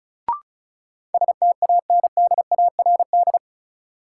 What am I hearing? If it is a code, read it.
Morse code, heard as a keyed tone, about 36 wpm, 700 Hz. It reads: STANDARD
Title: Sentences from Top 1000 Words - Review Part 6 - New Word Drill - 35wpm